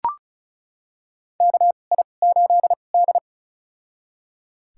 Morse code, heard as a keyed tone, about 36 wpm, spaced as for 25 wpm, 700 Hz. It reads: KI8D
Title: USA Call Signs - 4 Characters - 2x Character Spacing - 35wpm